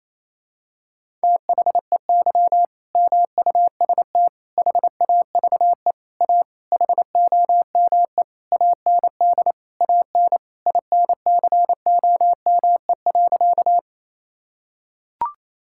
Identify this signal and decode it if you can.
Morse code, heard as a keyed tone, about 28 wpm, 700 Hz. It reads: THEY MUST HAVE A HOME AND AN INCOME.
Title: Sentences from Top 1000 Words - Review Part 5 - 28wpm